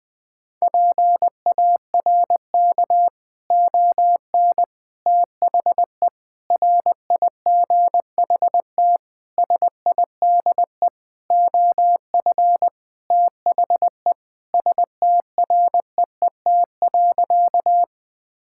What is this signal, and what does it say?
Morse code, heard as a keyed tone, about 20 wpm, 700 Hz. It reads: PARK ON THE RIGHT SIDE OF THE STREET.
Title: Sentences with 6-Word Phrases - 20wpm